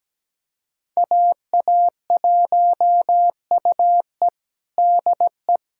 Morse code, heard as a keyed tone, about 17 wpm, 700 Hz. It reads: AA1UE DE
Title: USA Callsign DE Callsign - 5 Characters - 17wpm